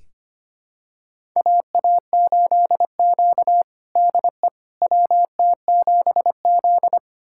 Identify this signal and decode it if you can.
Morse code, heard as a keyed tone, about 25 wpm, 700 Hz. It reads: AA8Q DE WT7Z
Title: USA Callsign DE Callsign - 4 Characters - 25wpm